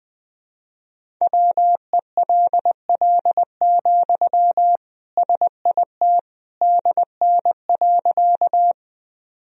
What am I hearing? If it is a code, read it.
Morse code, heard as a keyed tone, about 20 wpm, 700 Hz. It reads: WELL, SIT DN.
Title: Sentences from Top 500 Words and Abbreviations 20wpm